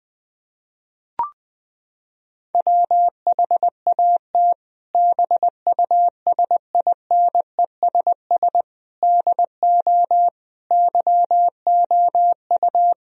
Morse code, heard as a keyed tone, about 20 wpm, 700 Hz. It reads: WHAT BUSINESS DO YOU
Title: Sentences from Top 1000 Words - Review Part 3 - 20wpm